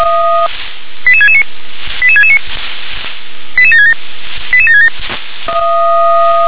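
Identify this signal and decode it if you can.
Five-tone selective call (ZVEI), heard as quick runs of five tones, tones of about 70 ms. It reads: 88669 88669 88767 88767
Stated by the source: Received with an SDR; ZVEI five-tone selective call.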